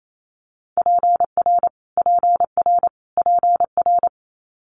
Morse code, heard as a keyed tone, about 28 wpm, 700 Hz. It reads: PR PR PR
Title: Multi-letter Phonemes Repeated 3x 28wpm